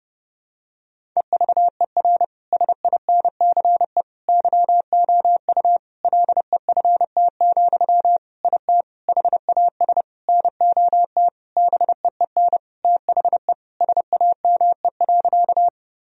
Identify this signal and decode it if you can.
Morse code, heard as a keyed tone, about 30 wpm, 700 Hz. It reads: EVER SINCE YOU LEFT, IT HAS NOT BEEN THE SAME.
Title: Sentences from Top 1000 Words - Review Part 1 - 30wpm